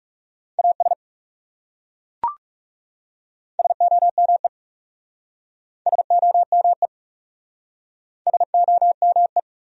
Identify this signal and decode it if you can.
Morse code, heard as a keyed tone, about 40 wpm, 700 Hz. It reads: AS SOME SOME SOME